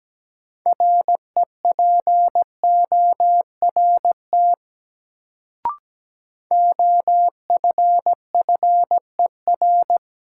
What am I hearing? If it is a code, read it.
Morse code, heard as a keyed tone, about 17 wpm, 700 Hz. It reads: REPORT OFFER
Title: Sentences from Top 900 Words - New Word Drill - 17wpm